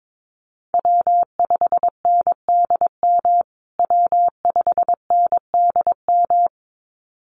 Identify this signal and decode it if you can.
Morse code, heard as a keyed tone, about 22 wpm, 700 Hz. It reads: W5NDM W5NDM
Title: USA Call Signs Repeated 2x 22wpm